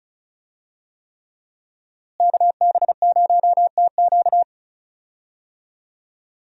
Morse code, heard as a keyed tone, about 36 wpm, 700 Hz. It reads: KB0TQ
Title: USA Call Signs - 5 Characters - Repeated 3x - 35wpm